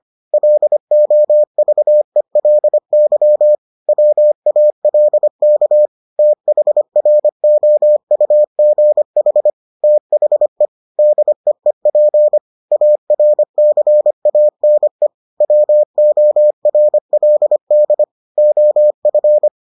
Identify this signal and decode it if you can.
Morse code, heard as a keyed tone, about 25 wpm, 600 Hz. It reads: LOVELY WALK THROUGH THE DEEP ARCANE WORLD OF